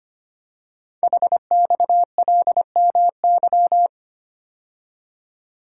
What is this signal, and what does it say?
Morse code, heard as a keyed tone, about 25 wpm, 700 Hz. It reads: HXLMY
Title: Code Groups - 5 Letters - 25wpm